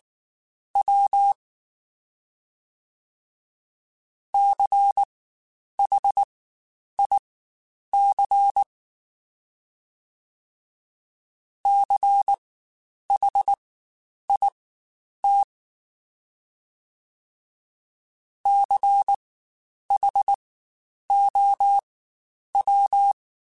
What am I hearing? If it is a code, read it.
Morse code, heard as a keyed tone, about 19 wpm, spaced as for 9 wpm, 800 Hz. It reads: W CHIC CHIT CHOW
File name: Spaced Alphas / Spaced Alphas mp3s / Cs sp @18